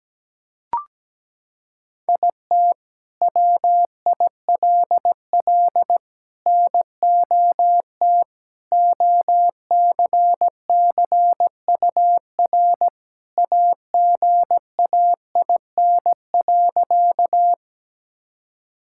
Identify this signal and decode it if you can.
Morse code, heard as a keyed tone, about 17 wpm, 700 Hz. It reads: IT WILL NOT OCCUR AGAIN.
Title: Sentences from Top 1000 Words - Review Part 6 - 17wpm